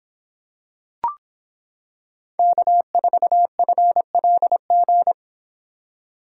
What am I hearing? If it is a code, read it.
Morse code, heard as a keyed tone, about 25 wpm, 700 Hz. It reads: K4FLG